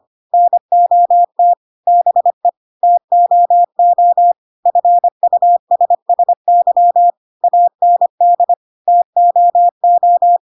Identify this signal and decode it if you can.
Morse code, heard as a keyed tone, about 25 wpm, 700 Hz. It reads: NOT BE TOO FUSSY AND TOO
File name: Pinocchio / Pinocchio ch7 25 25 ik1hge